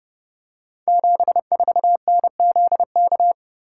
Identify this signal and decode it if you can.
Morse code, heard as a keyed tone, about 30 wpm, 700 Hz. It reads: 74NZK
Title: Code Groups - 5 Letter:Numbers - 30wpm